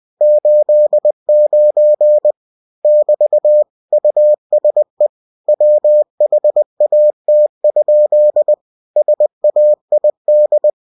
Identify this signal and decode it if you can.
Morse code, heard as a keyed tone, about 20 wpm, 600 Hz. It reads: 89 <BT> USE WHAT? SAID
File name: cw-books / winnie-the-pooh / winnie-20wpm-page-0089